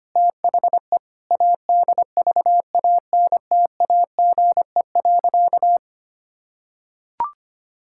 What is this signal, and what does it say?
Morse code, heard as a keyed tone, about 25 wpm, 700 Hz. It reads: THE ADVANTAGE.
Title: Sentences from Top 1000 Words - Review Part 6 - 25wpm